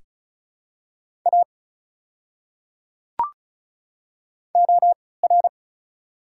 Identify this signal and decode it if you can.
Morse code, heard as a keyed tone, about 36 wpm, spaced as for 20 wpm, 700 Hz. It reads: A OR
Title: 3x Character Spacing - Top 200 Words - 35wpm